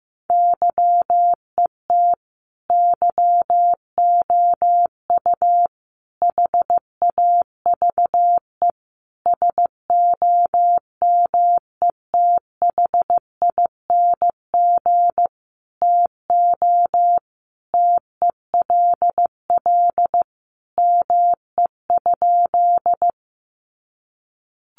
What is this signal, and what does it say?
Morse code, heard as a keyed tone, about 15 wpm, 700 Hz. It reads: YET YOU HAVE SOMETHING TO TELL ME?
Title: Sentences from Top 600 Words 15wpm